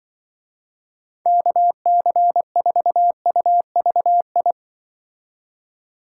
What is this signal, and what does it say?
Morse code, heard as a keyed tone, about 24 wpm, 700 Hz. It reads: KC4UVI
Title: Speed Racing - USA Call Signs 20wpm